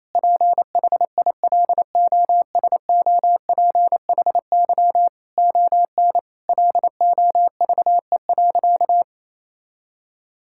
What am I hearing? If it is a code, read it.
Morse code, heard as a keyed tone, about 28 wpm, 700 Hz. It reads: PHILOSOPHY ON LOVE.
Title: Sentences from Top 1000 Words - Review Part 6 - 28wpm